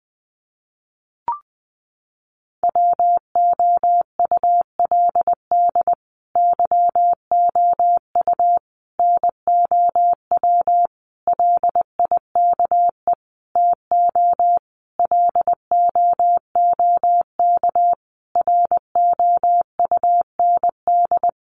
Morse code, heard as a keyed tone, about 20 wpm, 700 Hz. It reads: WOULD YOU NOW LIKE TO LOOK ROUND
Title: Sentences from Top 200 Words 20wpm